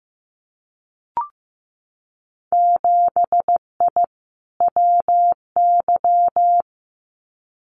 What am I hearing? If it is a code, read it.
Morse code, heard as a keyed tone, about 15 wpm, 700 Hz. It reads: 7I WY
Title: Winter Field Day Exchanges Encore 15wpm